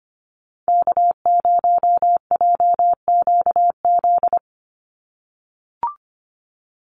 Morse code, heard as a keyed tone, about 25 wpm, 700 Hz. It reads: K0JQZ
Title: NA SOTA Activator Callsigns 25wpm